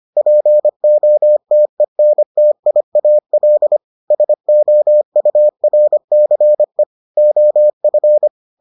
Morse code, heard as a keyed tone, about 25 wpm, 600 Hz. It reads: POTENTIAL SOURCE OF